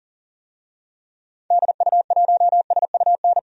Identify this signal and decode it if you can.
Morse code, heard as a keyed tone, about 40 wpm, 700 Hz. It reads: DU1SUN
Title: DXCC-100 Call Signs - 6 Characters - 40wpm